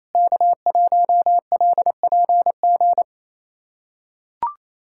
Morse code, heard as a keyed tone, about 28 wpm, 700 Hz. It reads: K1LPG
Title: USA Callsign DE Callsign - 5 Characters - 28wpm